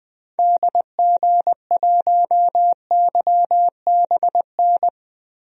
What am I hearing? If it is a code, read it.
Morse code, heard as a keyed tone, about 21 wpm, 700 Hz. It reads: DG1YBN
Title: DXCC-100 Call Signs - 6 Characters - Repeated 2x - 20wpm